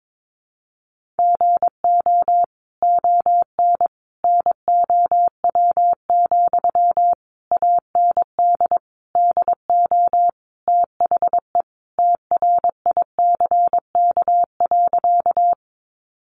Morse code, heard as a keyed tone, about 22 wpm, 700 Hz. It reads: GO ON NOW, AND DO THE TRICK.
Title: Sentences from Top 1500 Words 22wpm